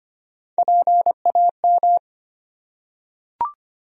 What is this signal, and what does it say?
Morse code, heard as a keyed tone, about 25 wpm, 700 Hz. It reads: PAM
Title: USA Common Names - Max 3 Letters - 25wpm